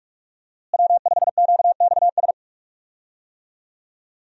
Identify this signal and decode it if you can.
Morse code, heard as a keyed tone, about 46 wpm, 700 Hz. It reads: W5QXS
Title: USA Call Signs - 5 Characters - Repeated 3x - 45wpm